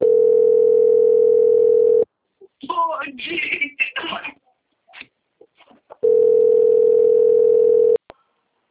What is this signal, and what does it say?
Ringback tone (North America), 440 + 480 Hz, with speech.
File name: out- - - - - .40056